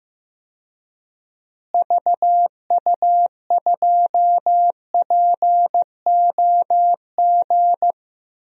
Morse code, heard as a keyed tone, about 15 wpm, 700 Hz. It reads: VU2POG
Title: DXCC-100 Call Signs - 6 Characters - Repeated 3x - 15wpm